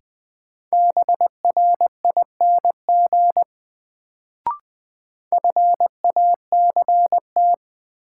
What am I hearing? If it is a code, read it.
Morse code, heard as a keyed tone, about 20 wpm, 700 Hz. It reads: BRING FACT
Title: Sentences from Top 500 Words - New Word Drill - 20wpm